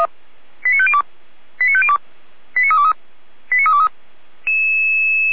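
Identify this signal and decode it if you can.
Five-tone selective call (ZVEI), heard as quick runs of five tones, tones of about 70 ms. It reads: 89502 89502 89323 89323
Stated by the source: ZVEI five-tone selective call 89015, received with an SDR.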